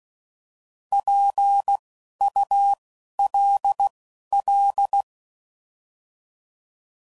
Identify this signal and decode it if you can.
Morse code, heard as a keyed tone, about 16 wpm, spaced as for 11 wpm, 800 Hz. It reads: PULL
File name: Prefixes / By First Letter / Ps